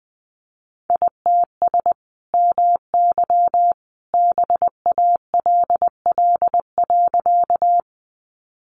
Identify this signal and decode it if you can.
Morse code, heard as a keyed tone, about 20 wpm, 700 Hz. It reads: ITS MY BALL.
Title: Sentences from Top 900 Words 20wpm